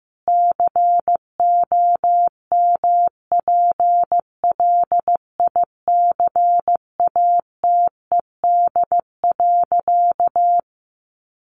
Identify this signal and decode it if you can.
Morse code, heard as a keyed tone, about 15 wpm, 700 Hz. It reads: COMPLICATED.
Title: Sentences from Top 1200 Words 15wpm